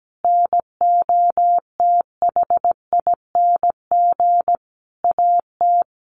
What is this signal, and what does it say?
Morse code, heard as a keyed tone, about 17 wpm, 700 Hz. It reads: NOTHING AT
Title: Sentences from Top 1500 Words 17wpm